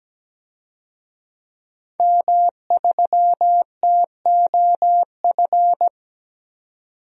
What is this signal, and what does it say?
Morse code, heard as a keyed tone, about 17 wpm, 700 Hz. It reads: M3TOF
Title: DXCC-100 Call Signs - 5 Characters - 17wpm